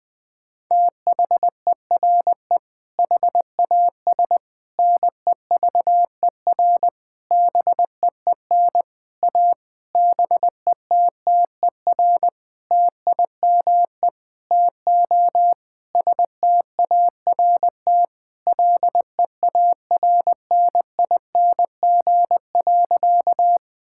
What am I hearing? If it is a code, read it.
Morse code, heard as a keyed tone, about 20 wpm, 700 Hz. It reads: THERE HAS NEVER BEEN A BETTER TIME TO START LEARNING.
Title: Sentences with 7-Word Phrases - 20wpm